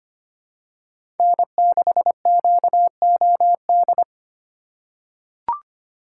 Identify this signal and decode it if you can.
Morse code, heard as a keyed tone, about 25 wpm, 700 Hz. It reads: N6QOD
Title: USA Call Signs - 2x Character Spacing - 25wpm